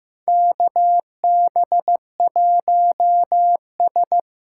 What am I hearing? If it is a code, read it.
Morse code, heard as a keyed tone, about 15 wpm, 700 Hz. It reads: KB1S
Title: USA Call Signs - 6 Characters - 8x Character Spacing - 15wpm